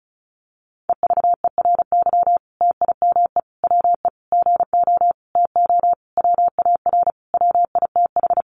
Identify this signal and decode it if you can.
Morse code, heard as a keyed tone, about 34 wpm, 700 Hz. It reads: EVERY TIME WE GO TO WAR WITH
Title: Sentences from Top 1300 Words 35wpm